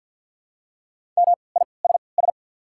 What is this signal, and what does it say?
Morse code, heard as a keyed tone, about 48 wpm, spaced as for 28 wpm, 690 Hz. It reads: MISS
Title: 3x Character Spacing - Top 500 Words - 50wpm